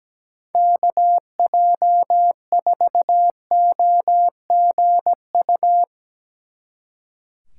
Morse code, heard as a keyed tone, about 18 wpm, 700 Hz. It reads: KJ4OGU
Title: Speed Racing - USA 2x3 Call Signs 17wpm